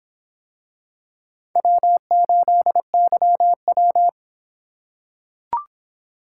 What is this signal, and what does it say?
Morse code, heard as a keyed tone, about 24 wpm, 700 Hz. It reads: W8YW